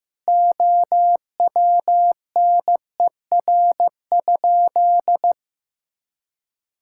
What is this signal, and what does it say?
Morse code, heard as a keyed tone, about 15 wpm, 700 Hz. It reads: OWNER?
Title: Sentences from Top 1600 Words 15wpm